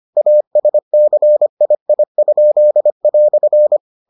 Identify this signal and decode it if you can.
Morse code, heard as a keyed tone, about 25 wpm, 600 Hz. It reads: ASCII?"